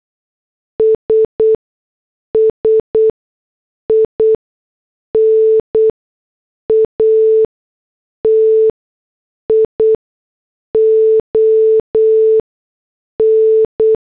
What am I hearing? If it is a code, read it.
Morse code, heard as a keyed tone, about 8 wpm, 440 Hz. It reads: S S I N A T I O N
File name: morse / headlines